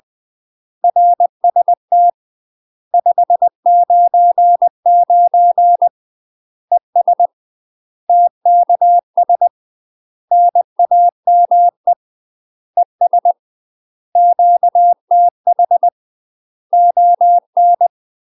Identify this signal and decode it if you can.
Morse code, heard as a keyed tone, about 20 wpm, 700 Hz. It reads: RST 599 ES TKS NAME ES QTH ON
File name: UK MASTER QSOs TRAINING / 20WPM / UK MASTER QSOs 0027